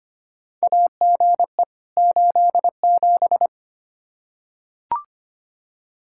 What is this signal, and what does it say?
Morse code, heard as a keyed tone, about 25 wpm, 700 Hz. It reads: AGE 87
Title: Practice QSO - Exchange 3 - More Info - 25wpm